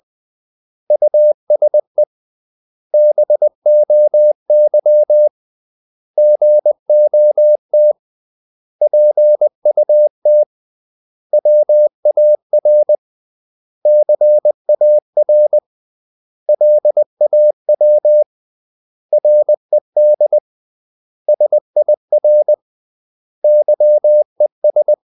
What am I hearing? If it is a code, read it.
Morse code, heard as a keyed tone, about 20 wpm, 600 Hz. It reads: USE BOY GOT PUT WAR CAR LAW RED SIR YES
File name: CW / Practice / 500MostCommonEnglishWords20